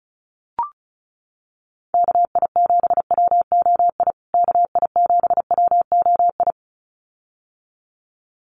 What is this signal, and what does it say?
Morse code, heard as a keyed tone, about 34 wpm, 700 Hz. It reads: KI7WOI KI7WOI
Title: USA Call Signs Repeated 2x 35wpm